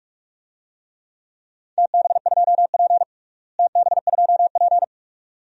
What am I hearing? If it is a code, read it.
Morse code, heard as a keyed tone, about 46 wpm, 700 Hz. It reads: TB2P TB2P
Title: DXCC-100 Call Signs - 4 Characters - Repeated 2x - 45wpm